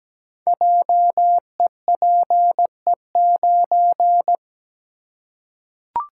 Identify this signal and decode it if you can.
Morse code, heard as a keyed tone, about 17 wpm, 700 Hz. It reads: JEPE9
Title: Code Groups - 5 Letter:Numbers - 17wpm